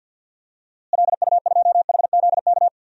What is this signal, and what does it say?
Morse code, heard as a keyed tone, about 48 wpm, 700 Hz. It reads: LU2HZK
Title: DXCC-100 Call Signs - 6 Characters - Repeated 3x - 50wpm